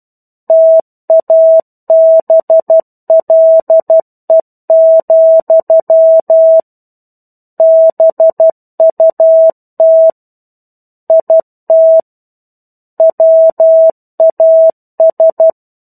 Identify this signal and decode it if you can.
Morse code, heard as a keyed tone, about 12 wpm, 650 Hz. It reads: TABLE, BUT IT WAS
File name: 01-Chapter-01 12 WPM